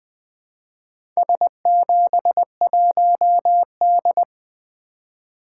Morse code, heard as a keyed tone, about 20 wpm, 700 Hz. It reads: S71D
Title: Four Letter-Number Combinations - 20wpm